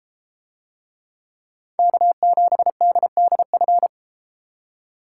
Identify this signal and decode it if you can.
Morse code, heard as a keyed tone, about 32 wpm, 700 Hz. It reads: K7DDF